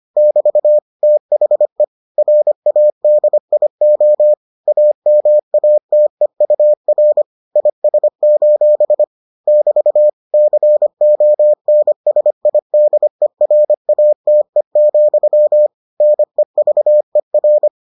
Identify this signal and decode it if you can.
Morse code, heard as a keyed tone, about 25 wpm, 600 Hz. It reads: <BT> THE RADIO AMATEUR IS: <BT> CONSIDERATE, NEVER